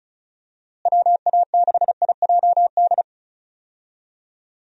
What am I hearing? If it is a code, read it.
Morse code, heard as a keyed tone, about 36 wpm, 700 Hz. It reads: WA6IJD
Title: USA Call Signs - 6 Characters - Repeated 3x - 35wpm